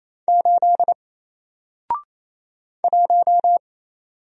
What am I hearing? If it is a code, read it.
Morse code, heard as a keyed tone, about 28 wpm, 700 Hz. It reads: 8 1
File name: Single-Letter-Number-28wpm